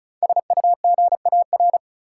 Morse code, heard as a keyed tone, about 36 wpm, 700 Hz. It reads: SUGAR
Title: Five Letter Words - 35wpm